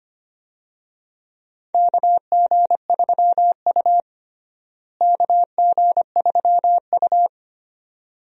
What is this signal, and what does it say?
Morse code, heard as a keyed tone, about 25 wpm, 700 Hz. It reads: KG3U KG3U
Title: USA Call Signs - 4 Characters - Repeated 2x - 25wpm